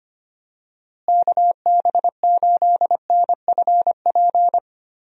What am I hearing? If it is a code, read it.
Morse code, heard as a keyed tone, about 25 wpm, 700 Hz. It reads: KB8NFP
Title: USA Call Signs - 6 Characters - Repeated 2x - 25wpm